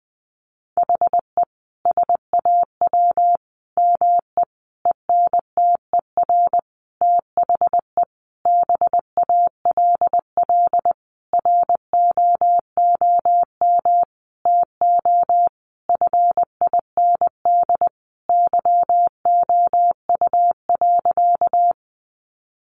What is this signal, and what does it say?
Morse code, heard as a keyed tone, about 20 wpm, 700 Hz. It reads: HE SAW ME ENTER THE BALL ROOM TO FIND YOU.
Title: Sentences from Top 1300 Words 20wpm